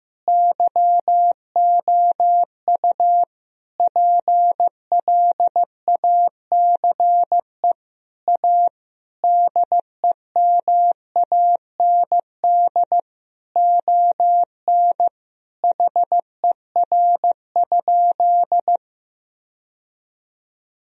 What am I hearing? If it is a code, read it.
Morse code, heard as a keyed tone, about 15 wpm, 700 Hz. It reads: YOU PLACE A DEMAND ON HER?
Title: Sentences from Top 1000 Words - Review Part 4 - 15wpm